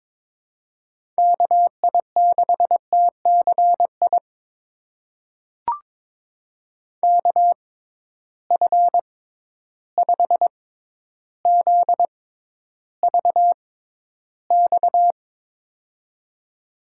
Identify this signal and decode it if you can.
Morse code, heard as a keyed tone, about 22 wpm, spaced as for 11 wpm, 700 Hz. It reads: KI6TCI KF5ZVX
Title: USA Call Signs - 6 Characters - 6x Character Spacing - 22wpm